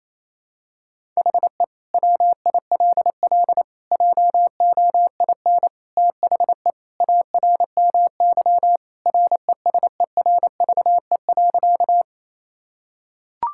Morse code, heard as a keyed tone, about 28 wpm, 700 Hz. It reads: HE WILL JOIN THE ARMY RESERVE.
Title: Sentences from Top 1000 Words - Review Part 6 - 28wpm